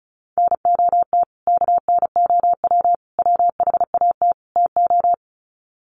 Morse code, heard as a keyed tone, about 34 wpm, 700 Hz. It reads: NOT KNOW WHAT TO
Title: Sets of 4 Words 35wpm